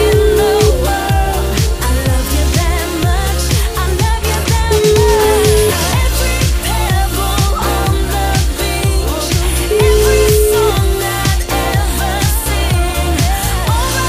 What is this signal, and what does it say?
Ringback tone (Europe, 425 Hz), with noise.